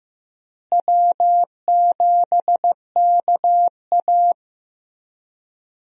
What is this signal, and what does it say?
Morse code, heard as a keyed tone, about 15 wpm, 700 Hz. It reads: W7KA
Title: USA Call Signs - 4 Characters - Repeated 2x - 15wpm